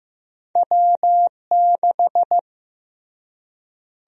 Morse code, heard as a keyed tone, about 15 wpm, 700 Hz. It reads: W6
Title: USA Call Sign Prefixes - Repeated 2x - 15wpm